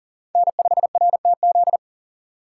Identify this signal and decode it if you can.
Morse code, heard as a keyed tone, about 40 wpm, 700 Hz. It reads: N5RTZ
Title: USA Call Signs - 5 Characters - 2x Character Spacing - 40wpm